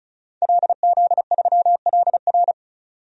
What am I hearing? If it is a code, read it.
Morse code, heard as a keyed tone, about 36 wpm, 700 Hz. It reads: LZ3LR
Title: DXCC-100 Call Signs - 5 Characters - 35wpm